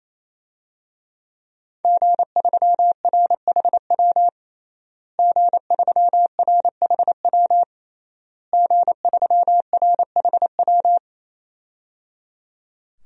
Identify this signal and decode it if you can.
Morse code, heard as a keyed tone, about 27 wpm, 700 Hz. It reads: G3RHW G3RHW G3RHW
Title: DXCC-100 Call Signs - 5 Characters - Repeated 3x - 28wpm